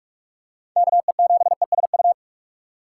Morse code, heard as a keyed tone, about 46 wpm, 700 Hz. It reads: KE7ESU
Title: USA Call Signs - 6 Characters - 4x Character Spacing - 45wpm